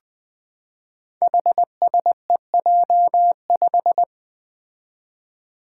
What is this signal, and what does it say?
Morse code, heard as a keyed tone, about 20 wpm, 700 Hz. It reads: HSEJ5
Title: Five Challenging Letter-Number Combinations 20wpm